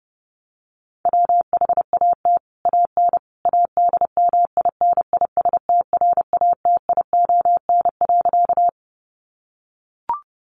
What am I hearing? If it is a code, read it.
Morse code, heard as a keyed tone, about 30 wpm, 700 Hz. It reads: WHAT AN ADMINISTRATION.
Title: Sentences from Top 1300 Words 30wpm